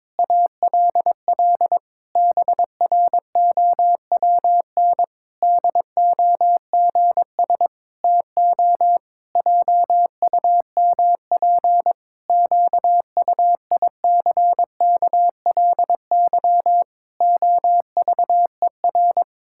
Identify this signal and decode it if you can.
Morse code, heard as a keyed tone, about 22 wpm, 700 Hz. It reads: ALL BROWN DOGS TO JUMP QUICKLY OVER
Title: Pangram Challenge v2 - 22wpm